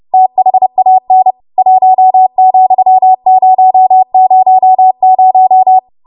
Morse code, heard as a keyed tone, about 30 wpm, 750 Hz. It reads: THAN 1,000
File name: Morse / 30WPM